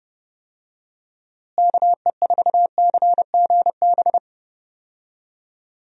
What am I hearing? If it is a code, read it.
Morse code, heard as a keyed tone, about 30 wpm, 700 Hz. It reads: KE4CGB
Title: USA Call Signs - 6 Characters - Repeated 3x - 30wpm